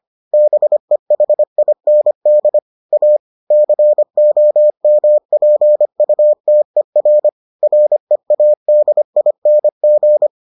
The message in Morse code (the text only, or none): BEHIND A COMPUTER READING